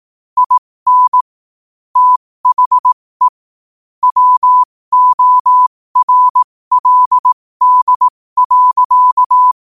Morse code, heard as a keyed tone, about 18 wpm, spaced as for 15 wpm, 1000 Hz. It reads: IN THE WORLD.
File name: qotdfiles / 15wpm